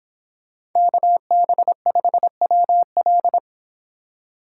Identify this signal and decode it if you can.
Morse code, heard as a keyed tone, about 26 wpm, 700 Hz. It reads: KB5WL